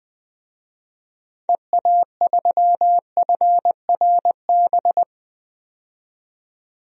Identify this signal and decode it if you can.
Morse code, heard as a keyed tone, about 20 wpm, 700 Hz. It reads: EA3FRB
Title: DXCC-100 Call Signs - 6 Characters - 20wpm